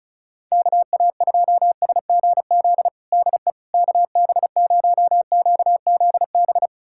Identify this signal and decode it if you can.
Morse code, heard as a keyed tone, about 36 wpm, 700 Hz. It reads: KA2SGZ DE KB0QZB
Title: USA Callsign DE Callsign 35wpm